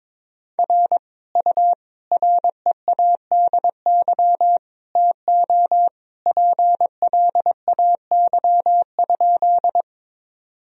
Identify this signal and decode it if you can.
Morse code, heard as a keyed tone, about 22 wpm, 700 Hz. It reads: R U READY TO PLAY?
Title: Sentences from Top 400 Words and Abbreviations 22wpm